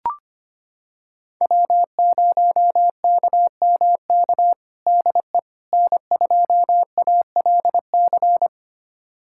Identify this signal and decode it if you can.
Morse code, heard as a keyed tone, about 25 wpm, 700 Hz. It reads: W0KMK DE N2ALC
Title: USA Callsign DE Callsign - 5 Characters - 25wpm